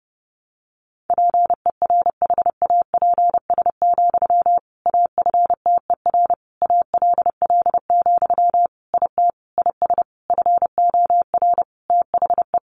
Morse code, heard as a keyed tone, about 30 wpm, 700 Hz. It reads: PERHAPS, AFTER ALL, IT IS FOR THE
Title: Sentences from Top 500 Words 30wpm